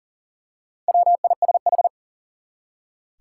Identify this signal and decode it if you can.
Morse code, heard as a keyed tone, about 40 wpm, 700 Hz. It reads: WISH
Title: Four Letter Words - 40wpm